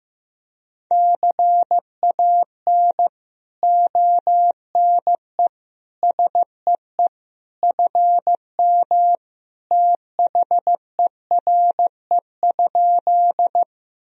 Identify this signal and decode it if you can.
Morse code, heard as a keyed tone, about 15 wpm, 700 Hz. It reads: CAN ONE SEE FM THERE?
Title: Sentences from Top 100 Words and Abbreviations 15wpm